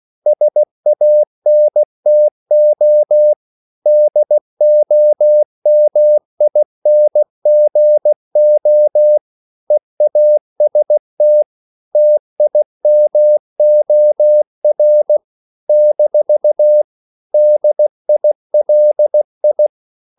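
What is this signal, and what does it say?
Morse code, heard as a keyed tone, about 16 wpm, 600 Hz. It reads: SANTO DOMINGO EAST TIMOR - DILI
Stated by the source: CW practice of national capitals at 16 words per minute, from ON6ZQ.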